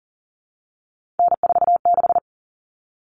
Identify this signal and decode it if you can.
Morse code, heard as a keyed tone, about 40 wpm, 700 Hz. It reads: N46
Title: All Cut Numbers - 3-Digit 40wpm